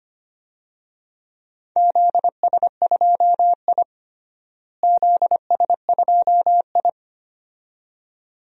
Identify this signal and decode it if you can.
Morse code, heard as a keyed tone, about 25 wpm, 700 Hz. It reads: ZS2I ZS2I
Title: DXCC-100 Call Signs - 4 Characters - Repeated 2x - 25wpm